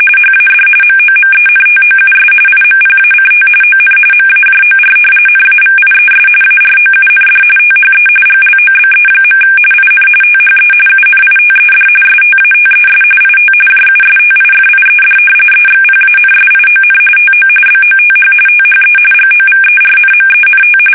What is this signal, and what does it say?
RTTY, about 45.45 baud. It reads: HAD MOMENTOUS IMPACT ON MAJOR NAVIES AROUND THE GLOBE. IN THE UNITED STATES IT JUSTIFIED EXPANSION TO BOTH THE GOVERNMENT
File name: fsk850 / usnavy3